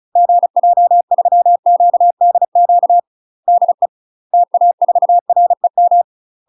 Morse code, heard as a keyed tone, about 34 wpm, 700 Hz. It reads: GJ3QDQ DE TA4REM
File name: UK MASTER QSOs TRAINING / 35WPM / UK MASTER QSOs 0043